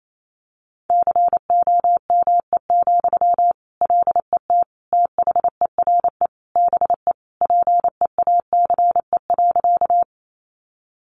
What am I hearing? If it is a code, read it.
Morse code, heard as a keyed tone, about 28 wpm, 700 Hz. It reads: COME, LET THERE BE PEACE.
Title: Sentences from Top 1500 Words 28wpm